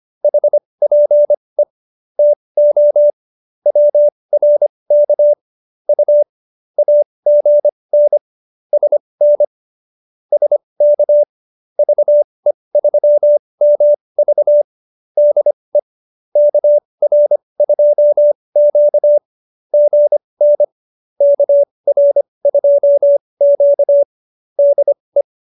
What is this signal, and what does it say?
Morse code, heard as a keyed tone, about 25 wpm, spaced as for 20 wpm, 590 Hz. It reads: HPE TO WRK U AGN SN SK VE3MV DE KR2Q GN KR2Q DE
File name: qso220b 20